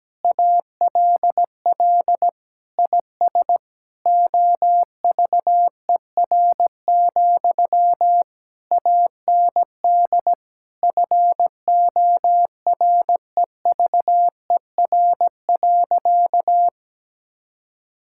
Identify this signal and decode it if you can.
Morse code, heard as a keyed tone, about 17 wpm, 700 Hz. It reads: ALL IS OVER, AND FOREVER.
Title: Sentences from Top 1600 Words 17wpm